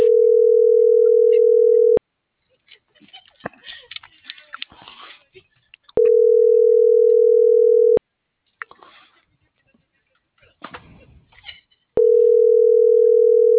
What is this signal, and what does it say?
Ringback tone (North America), 440 + 480 Hz.